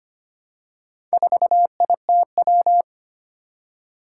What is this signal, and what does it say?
Morse code, heard as a keyed tone, about 25 wpm, 700 Hz. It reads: 4ITW
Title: Four Challenging Letter-Number Combinations 25wpm